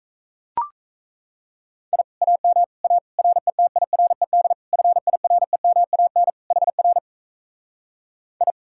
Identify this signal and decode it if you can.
Morse code, heard as a keyed tone, about 37 wpm, 700 Hz. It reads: I AM A RETIRED FIREMAN HR I